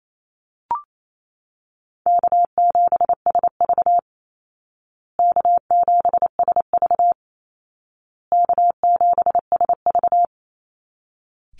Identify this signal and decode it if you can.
Morse code, heard as a keyed tone, about 28 wpm, 700 Hz. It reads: K7SV K7SV K7SV
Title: Easy Call Signs Repeated 3x - 28wpm